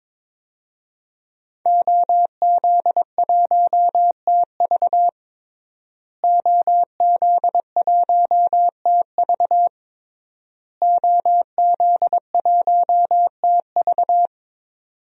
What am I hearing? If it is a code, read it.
Morse code, heard as a keyed tone, about 22 wpm, 700 Hz. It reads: OZ1TV OZ1TV OZ1TV
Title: DXCC-100 Call Signs - 5 Characters - Repeated 3x - 22wpm